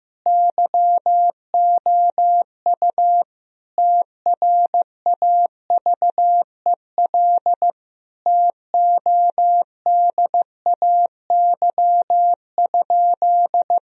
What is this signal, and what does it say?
Morse code, heard as a keyed tone, about 15 wpm, 700 Hz. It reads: YOU TRAVEL TODAY?
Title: Sentences from Top 1000 Words - Review Part 2 - 15wpm